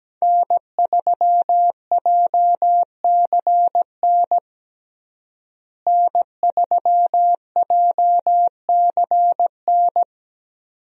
Morse code, heard as a keyed tone, about 17 wpm, 700 Hz. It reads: N3JCN N3JCN
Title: USA Call Signs - 5 Characters - Repeated 2x - 17wpm